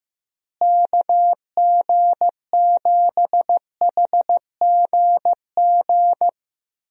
Morse code, heard as a keyed tone, about 15 wpm, 700 Hz. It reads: KG7HGG
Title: USA Call Signs - 6 Characters - Repeated 3x - 15wpm